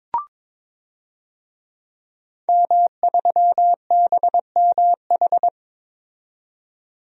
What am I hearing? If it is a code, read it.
Morse code, heard as a keyed tone, about 21 wpm, 700 Hz. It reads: M3BMH
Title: DXCC-100 Call Signs - 5 Characters - 22wpm